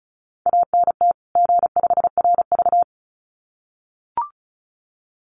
Morse code, heard as a keyed tone, about 34 wpm, 700 Hz. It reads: ANT G5RV
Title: Common QSO Phrases 35wpm